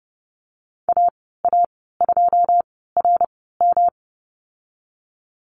Morse code, heard as a keyed tone, about 30 wpm, spaced as for 17 wpm, 700 Hz. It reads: AA2RM
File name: Lesson 19g - Letter w - Call Signs 30wpm